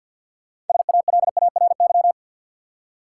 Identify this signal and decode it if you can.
Morse code, heard as a keyed tone, about 52 wpm, 700 Hz. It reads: SALARY
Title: Six Letter Words - 50wpm